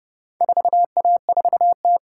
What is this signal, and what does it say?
Morse code, heard as a keyed tone, about 29 wpm, 700 Hz. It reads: 4A4T